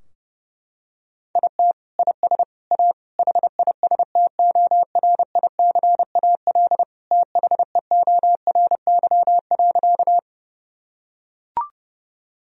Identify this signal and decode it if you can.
Morse code, heard as a keyed tone, about 30 wpm, 700 Hz. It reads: IT IS A HISTORICAL THEORY.
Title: Sentences from Top 1000 Words - Review Part 4 - 30wpm